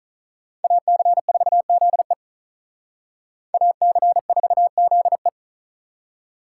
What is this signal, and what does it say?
Morse code, heard as a keyed tone, about 39 wpm, 700 Hz. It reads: AC4ZE AC4ZE
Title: Speed Racing - USA Call Signs 35wpm